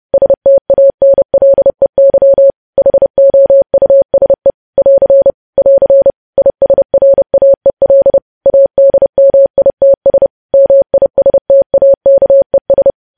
Morse code, heard as a keyed tone, about 30 wpm, 550 Hz. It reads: STANLEY HOUSE <AR> <AR> ISRAEL ADMITS MISTAKES
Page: news headlines in morse code at 30 wpm